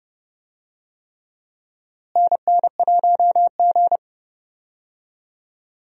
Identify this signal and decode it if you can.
Morse code, heard as a keyed tone, about 31 wpm, 700 Hz. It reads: NN1G
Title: USA Call Signs - 4 Characters - Repeated 3x - 30wpm